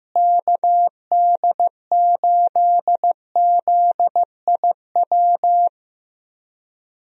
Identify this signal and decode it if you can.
Morse code, heard as a keyed tone, about 15 wpm, 700 Hz. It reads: KD8ZIW
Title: USA Call Signs - 6 Characters - Repeated 2x - 15wpm